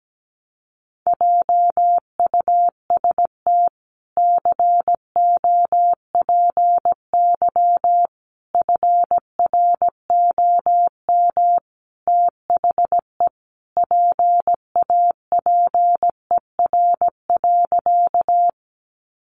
Morse code, heard as a keyed tone, about 17 wpm, 700 Hz. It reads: JUST COPY FROM THE PAPER.
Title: Sentences from Top 1400 Words 17wpm